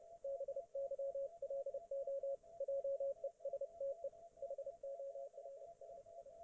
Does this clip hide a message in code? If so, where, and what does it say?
Yes: Morse code from 0.2 to 6.3 s, heard as a keyed tone, about 30 wpm, 550 Hz, with radio noise. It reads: BYLO JESTE HORSI